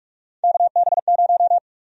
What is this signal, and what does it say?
Morse code, heard as a keyed tone, about 46 wpm, 700 Hz. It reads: KB0
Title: USA Call Sign Prefixes - Repeated 2x - 45wpm